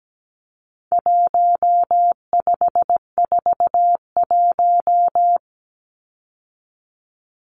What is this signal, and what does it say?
Morse code, heard as a keyed tone, about 17 wpm, 700 Hz. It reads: 1541
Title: Numbers - 4-Digit 17wpm